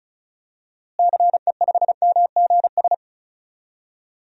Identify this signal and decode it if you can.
Morse code, heard as a keyed tone, about 34 wpm, 700 Hz. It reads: CE5MGS
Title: DXCC-100 Call Signs - 6 Characters - Repeated 3x - 35wpm